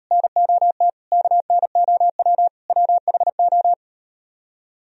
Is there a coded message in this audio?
Yes: Morse code from 0.1 to 3.7 s, heard as a keyed tone, about 38 wpm, 700 Hz. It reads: NOT KNOW WHO